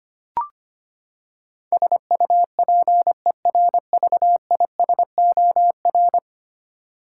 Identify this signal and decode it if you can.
Morse code, heard as a keyed tone, about 25 wpm, 700 Hz. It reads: SUPERVISOR
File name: practice / 25wpm / Ten Letter Words - 25wpm